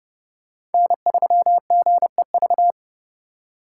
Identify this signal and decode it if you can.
Morse code, heard as a keyed tone, about 32 wpm, 700 Hz. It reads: N3GEV
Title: Code Groups - 5 Letter:Numbers - 30wpm